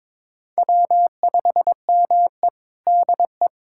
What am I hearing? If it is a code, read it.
Morse code, heard as a keyed tone, about 22 wpm, 700 Hz. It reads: W5ME DE
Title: USA Callsign DE Callsign - 4 Characters - 22wpm